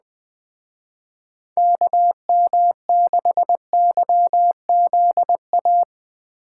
Morse code, heard as a keyed tone, about 19 wpm, 700 Hz. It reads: KM6YZA
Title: Speed Racing - USA Call Signs 17wpm